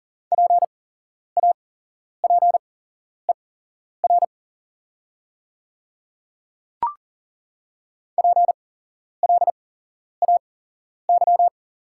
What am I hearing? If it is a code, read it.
Morse code, heard as a keyed tone, about 40 wpm, spaced as for 11 wpm, 690 Hz. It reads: PAPER PLAY
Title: 8x Character Spacing - Top 300 Words - 40wpm